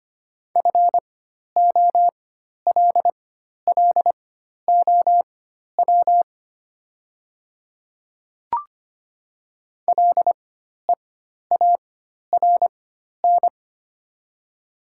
Morse code, heard as a keyed tone, about 25 wpm, spaced as for 12 wpm, 700 Hz. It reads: FOLLOW LEARN
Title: 4x Character Spacing - Top 300 Words - 25wpm